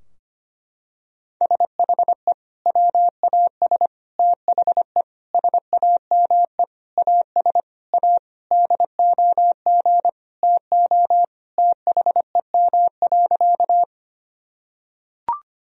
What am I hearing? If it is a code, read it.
Morse code, heard as a keyed tone, about 25 wpm, 700 Hz. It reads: SHE WAS THE SAME AS A DOG TO THEM.
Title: Sentences from Top 1000 Words - Review Part 2 - 25wpm